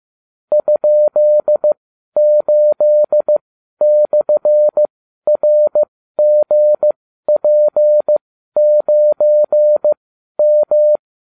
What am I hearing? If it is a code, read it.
Morse code, heard as a keyed tone, about 15 wpm, spaced as for 12 wpm, 600 Hz. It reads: ?8/RGP9M